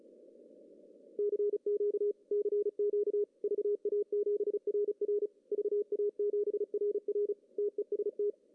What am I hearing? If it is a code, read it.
Morse code, heard as a keyed tone, about 33 wpm, 410 Hz. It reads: CQ CQ VA7RR VA7RR TEST
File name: CW / cqww-cw-k5tr